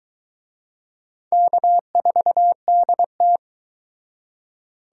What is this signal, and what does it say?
Morse code, heard as a keyed tone, about 24 wpm, 700 Hz. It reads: K4DT